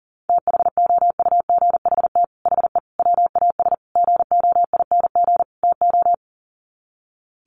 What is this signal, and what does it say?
Morse code, heard as a keyed tone, about 40 wpm, 700 Hz. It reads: THOUGHT HE WAS GOING TO
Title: Sets of 5 Words Encore2 40wpm